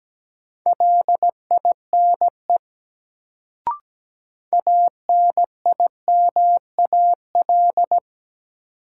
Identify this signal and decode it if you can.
Morse code, heard as a keyed tone, about 17 wpm, 700 Hz. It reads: LINE ANIMAL
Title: Sentences from Top 1000 Words - Review Part 1 - New Word Drill - 17wpm